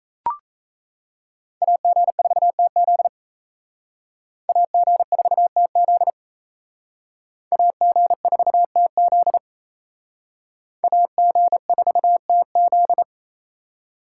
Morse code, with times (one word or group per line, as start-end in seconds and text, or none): 1.6-3.1 AG4TZ
4.5-6.1 AG4TZ
7.5-9.4 AG4TZ
10.8-13.0 AG4TZ